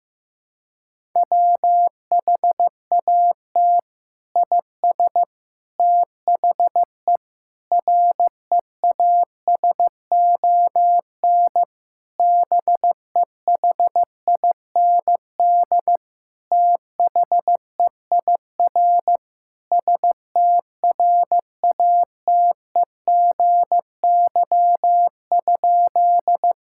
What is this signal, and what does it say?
Morse code, heard as a keyed tone, about 15 wpm, 700 Hz. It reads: WHAT IS THE REASON BEHIND THEIR STRATEGY?
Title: Sentences from Top 1000 Words - Review Part 5 - 15wpm